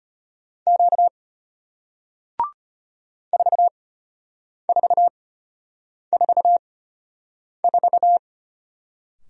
Morse code, heard as a keyed tone, about 34 wpm, spaced as for 8 wpm, 700 Hz. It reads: Q 4444